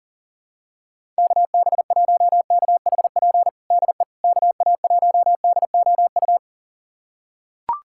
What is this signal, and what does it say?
Morse code, heard as a keyed tone, about 37 wpm, 700 Hz. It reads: KB1KHP DE KA1DOU
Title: USA Callsign DE Callsign 40wpm